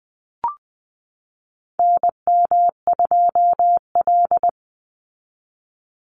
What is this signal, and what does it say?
Morse code, heard as a keyed tone, about 20 wpm, 700 Hz. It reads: NM2L
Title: Call Signs Repeated 3x - 20wpm